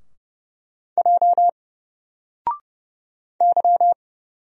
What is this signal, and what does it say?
Morse code, heard as a keyed tone, about 29 wpm, 700 Hz. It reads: J Y
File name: Single-Letter-Number-30wpm